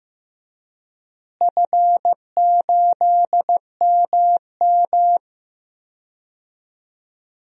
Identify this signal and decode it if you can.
Morse code, heard as a keyed tone, about 15 wpm, 700 Hz. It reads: F8MM
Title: DXCC-100 Call Signs - 4 Characters - Repeated 2x - 15wpm